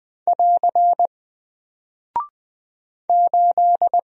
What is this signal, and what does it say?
Morse code, heard as a keyed tone, about 20 wpm, 700 Hz. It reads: <AR> 8
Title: Single Character (Letter:Number:Punctuation:Prosign) Alt 20wpm